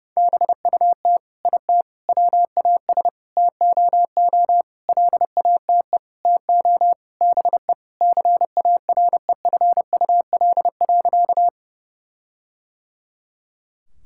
Morse code, heard as a keyed tone, about 32 wpm, 700 Hz. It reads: BUT IT WAS TOO LATE TO BE CAREFUL.
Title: Sentences from Top 1600 Words 30wpm